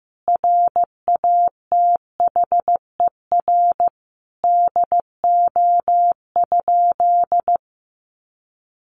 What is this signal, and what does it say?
Morse code, heard as a keyed tone, about 15 wpm, 700 Hz. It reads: RATHER DO?
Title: Sentences from Top 700 Words 15wpm